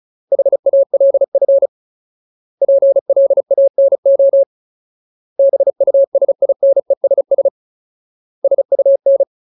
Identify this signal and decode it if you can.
Morse code, heard as a keyed tone, about 34 wpm, 550 Hz. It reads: HALF PLANO BUSINESS SUN
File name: Ultimate Word List / 35WPM Ultimate